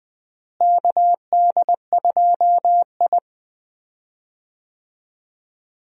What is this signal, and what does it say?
Morse code, heard as a keyed tone, about 20 wpm, 700 Hz. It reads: KD2I
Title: USA Call Signs - 4 Characters - Repeated 2x - 20wpm